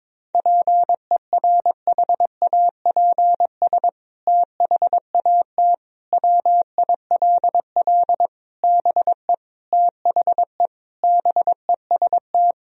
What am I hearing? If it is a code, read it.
Morse code, heard as a keyed tone, about 22 wpm, 700 Hz. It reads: PERHAPS THAT WILL BE THE BEST
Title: Sentences from Top 1000 Words - Review Part 2 - 22wpm